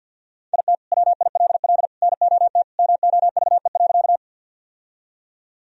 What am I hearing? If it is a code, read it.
Morse code, heard as a keyed tone, about 48 wpm, 700 Hz. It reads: IT WILL NOT MOVE.
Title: Sentences from Top 1000 Words - Review Part 1 - 50wpm